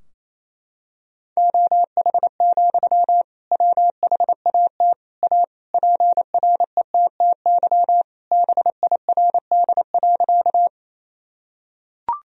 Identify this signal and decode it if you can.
Morse code, heard as a keyed tone, about 28 wpm, 690 Hz. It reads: OH, WHAT A PRETTY BIRD.
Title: Sentences from Top 1000 Words - Review Part 2 - 28wpm